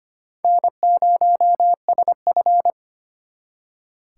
Morse code, heard as a keyed tone, about 33 wpm, 700 Hz. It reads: N0SF